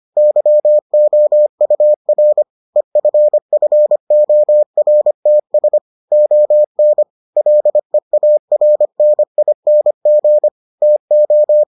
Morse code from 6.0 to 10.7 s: ON LEARNING

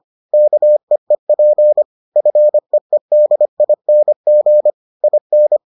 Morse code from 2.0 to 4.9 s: FEEDING

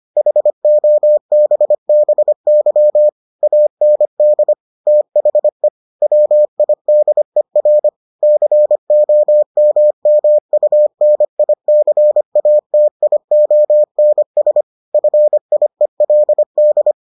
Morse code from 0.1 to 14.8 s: HOBBY AND THE WIDER COMMUNICATIONS

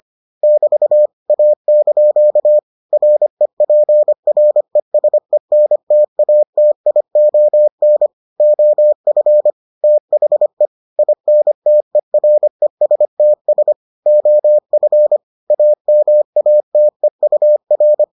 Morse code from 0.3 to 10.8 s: <BT> A) REPRESENTATION OF THE

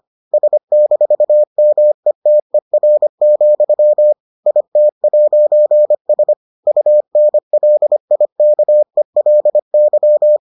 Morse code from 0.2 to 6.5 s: S-METER, IT'S

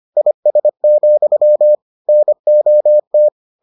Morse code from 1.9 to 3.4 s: NOT

